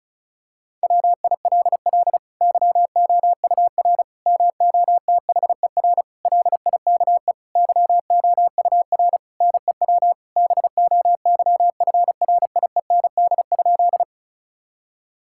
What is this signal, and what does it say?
Morse code, heard as a keyed tone, about 36 wpm, 700 Hz. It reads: WILL YOUR MOTHER LIKE YOUR NEW BOYFRIEND?
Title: Sentences from Top 1000 Words - Review Part 5 - 35wpm